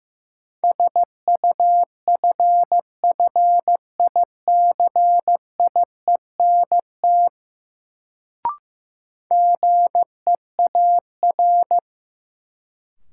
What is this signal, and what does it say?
Morse code, heard as a keyed tone, about 15 wpm, 700 Hz. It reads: SUFFICIENT GEAR
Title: Sentences from Top 1400 Words - New Word Drill Alt - 15wpm